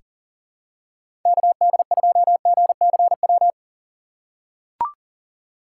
Morse code, heard as a keyed tone, about 40 wpm, 700 Hz. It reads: KD2GCW
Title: USA Call Signs - 6 Characters - 2x Character Spacing - 40wpm